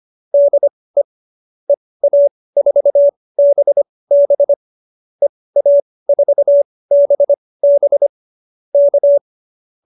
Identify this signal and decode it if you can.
Morse code, heard as a keyed tone, about 25 wpm, spaced as for 18 wpm, 590 Hz. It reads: DE EA4BB EA4BB K
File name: qso215b 18